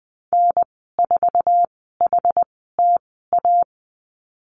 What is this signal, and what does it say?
Morse code, heard as a keyed tone, about 20 wpm, spaced as for 14 wpm, 700 Hz. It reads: N4HTA
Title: 062 Lesson 11g - Letter h - Call Signs 20wpm